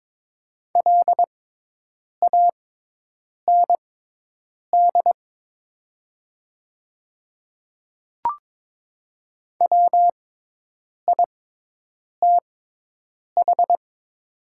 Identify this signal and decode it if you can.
Morse code, heard as a keyed tone, about 22 wpm, spaced as for 8 wpm, 700 Hz. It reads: LAND WITH
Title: 6x Character Spacing - Top 500 Words - 22wpm